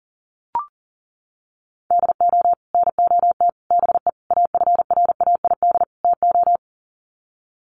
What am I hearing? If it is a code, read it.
Morse code, heard as a keyed tone, about 40 wpm, 700 Hz. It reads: DO NOT BE AFRAID TO
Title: Sets of 5 Words Encore2 40wpm